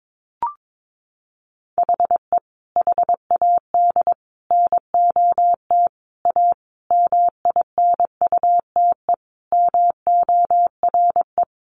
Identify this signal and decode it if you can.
Morse code, heard as a keyed tone, about 22 wpm, 700 Hz. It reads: HE HAD NOT A MINUTE MORE
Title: Sentences from Top 1100 Words 22wpm